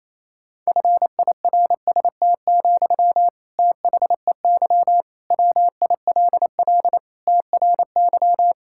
Morse code, heard as a keyed tone, about 28 wpm, 700 Hz. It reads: FIRST, THEY WILL TRY
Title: Sentences from Top 1000 Words - Review Part 6 - 28wpm